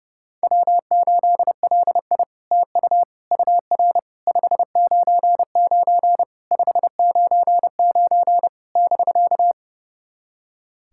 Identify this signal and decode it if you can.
Morse code, heard as a keyed tone, about 30 wpm, 700 Hz. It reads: W8LI TU UR 599 599 <BK>
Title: SOTA QSO Practice 30wpm